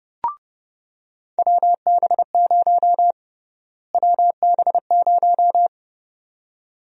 Morse code, heard as a keyed tone, about 32 wpm, 700 Hz. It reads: WB0 WB0
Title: USA Call Sign Prefixes - Repeated 2x - 30wpm